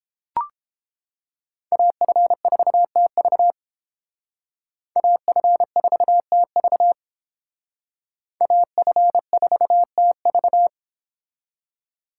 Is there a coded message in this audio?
Yes: Morse code from 1.7 to 10.7 s, heard as a keyed tone, about 29 wpm, 700 Hz. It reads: AF4TV AF4TV AF4TV